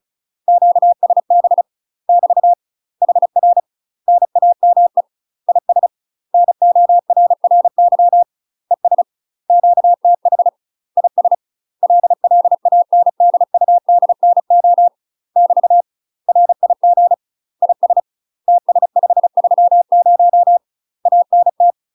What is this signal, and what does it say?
Morse code, heard as a keyed tone, about 36 wpm, 700 Hz. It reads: QSB <BT> HR NAME IS NORRY ES QTH IS LLANDUDNO <BT> RIG IS TS530 ANT
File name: UK MASTER QSOs TRAINING / 35WPM / UK MASTER QSOs 0027